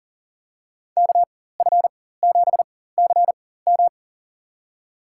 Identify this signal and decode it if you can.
Morse code, heard as a keyed tone, about 40 wpm, spaced as for 19 wpm, 700 Hz. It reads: KF7CM
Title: USA Call Signs - 5 Characters - 4x Character Spacing - 40wpm